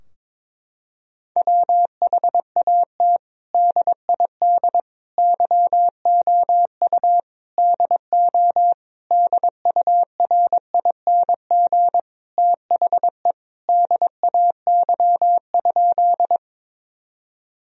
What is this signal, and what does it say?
Morse code, heard as a keyed tone, about 22 wpm, 700 Hz. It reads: WHAT DID YOU DO DURING THE DAY?
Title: Sentences from Top 1000 Words - Review Part 2 - 22wpm